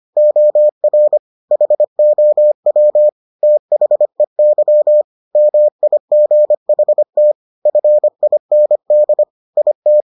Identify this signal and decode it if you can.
Morse code, heard as a keyed tone, about 25 wpm, 600 Hz. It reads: OR HOW THEY MIGHT FIND IT